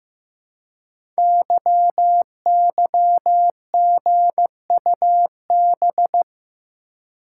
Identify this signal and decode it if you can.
Morse code, heard as a keyed tone, about 15 wpm, 700 Hz. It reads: YYGUB
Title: Code Groups - 5 Letters - 15wpm